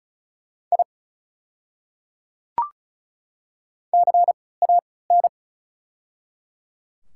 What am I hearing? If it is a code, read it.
Morse code, heard as a keyed tone, about 36 wpm, spaced as for 20 wpm, 700 Hz. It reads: I CAN
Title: 3x Character Spacing - Top 200 Words - 35wpm